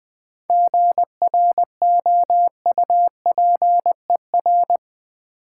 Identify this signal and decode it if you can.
Morse code, heard as a keyed tone, about 20 wpm, 700 Hz. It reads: GROUPER
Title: Difficult Words with Hints - 20wpm